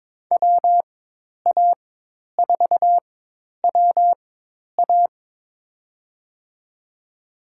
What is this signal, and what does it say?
Morse code, heard as a keyed tone, about 22 wpm, spaced as for 10 wpm, 700 Hz. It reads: WA4WA
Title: USA Call Signs - 4x Character Spacing - 22wpm